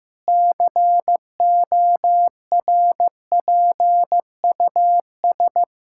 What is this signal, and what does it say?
Morse code, heard as a keyed tone, about 15 wpm, 700 Hz. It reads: CORPUS
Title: Practice QSO - Exchange 1 - Beginning - 15wpm